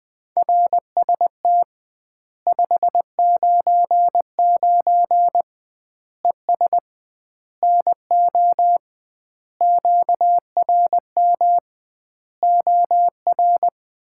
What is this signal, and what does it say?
Morse code, heard as a keyed tone, about 20 wpm, 700 Hz. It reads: RST 599 ES NO QRM OR
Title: Common QSO Phrases - 2x Word-Spacing - 20wpm